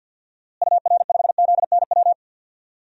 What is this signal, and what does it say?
Morse code, heard as a keyed tone, about 48 wpm, 700 Hz. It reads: UR5ZNW
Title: DXCC-100 Call Signs - 6 Characters - Repeated 3x - 50wpm